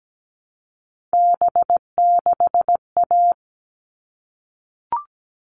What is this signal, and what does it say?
Morse code, heard as a keyed tone, about 17 wpm, 700 Hz. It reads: B6A
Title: All Cut Numbers - 3-Digit 17wpm